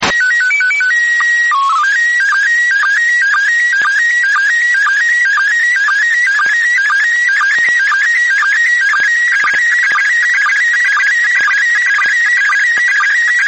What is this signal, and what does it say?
SSTV (slow-scan television), PD 120.
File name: blog / mypics / iss sstv